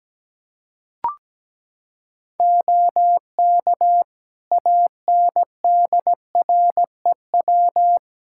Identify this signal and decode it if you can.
Morse code, heard as a keyed tone, about 17 wpm, 700 Hz. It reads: OK ANDREW
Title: Practice QSO - Exchange 3 - More Info - 17wpm